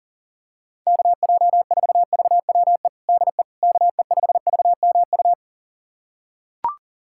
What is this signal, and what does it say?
Morse code, heard as a keyed tone, about 40 wpm, 700 Hz. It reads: KJ4VWE DE KE5VMU
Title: USA Callsign DE Callsign - 6 Characters - 40wpm